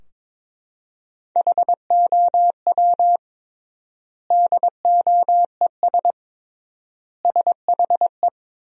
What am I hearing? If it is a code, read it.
Morse code, heard as a keyed tone, about 22 wpm, 700 Hz. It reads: HOW DOES SHE
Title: 3x Word-Spacing - Sentences from Top 200 Words - 22wpm